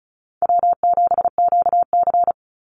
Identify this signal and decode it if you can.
Morse code, heard as a keyed tone, about 34 wpm, 700 Hz. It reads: W7QC
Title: Easy Call Signs Repeated 3x - 35wpm